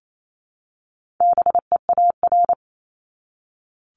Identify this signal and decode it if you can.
Morse code, heard as a keyed tone, about 28 wpm, 700 Hz. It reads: BEAR
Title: ING Suffix - Top 100 Words with Base Words 28wpm